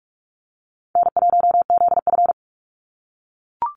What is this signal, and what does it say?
Morse code, heard as a keyed tone, about 44 wpm, 700 Hz. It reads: N1ZF
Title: NA SOTA Activator Callsigns 45wpm